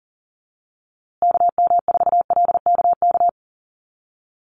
Morse code, heard as a keyed tone, about 40 wpm, 700 Hz. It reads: KM4LKK
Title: USA Call Signs Repeated 2x 40wpm